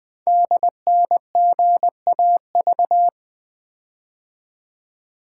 Morse code, heard as a keyed tone, about 20 wpm, 700 Hz. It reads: DNGAV
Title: Code Groups - 5 Letters - 20wpm